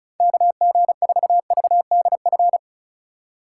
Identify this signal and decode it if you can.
Morse code, heard as a keyed tone, about 44 wpm, 700 Hz. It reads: KG4VDF